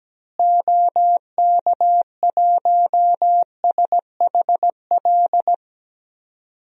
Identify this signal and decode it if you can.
Morse code, heard as a keyed tone, about 17 wpm, 700 Hz. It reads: OK1SHL
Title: DXCC-100 Call Signs - 6 Characters - Repeated 3x - 17wpm